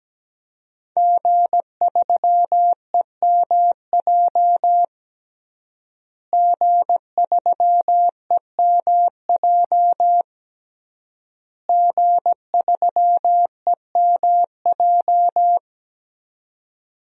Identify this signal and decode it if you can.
Morse code, heard as a keyed tone, about 17 wpm, 700 Hz. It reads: G3EMJ G3EMJ G3EMJ
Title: DXCC-100 Call Signs - 5 Characters - Repeated 3x - 17wpm